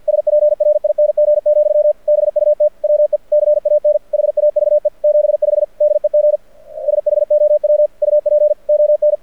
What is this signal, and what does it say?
Morse code, heard as a keyed tone, about 53 wpm, 600 Hz. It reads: SPRING, BUT WE CAN HAVE 75 DEG SHOW UP ON
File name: archived / CRSnet